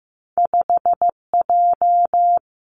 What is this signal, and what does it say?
Morse code, heard as a keyed tone, about 15 wpm, 700 Hz. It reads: 5J
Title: Two Challenging Letter-Number Combinations 15wpm